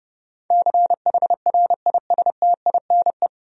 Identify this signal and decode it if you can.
Morse code, heard as a keyed tone, about 30 wpm, 700 Hz. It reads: CHRISTINE
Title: USA Common Names - 30wpm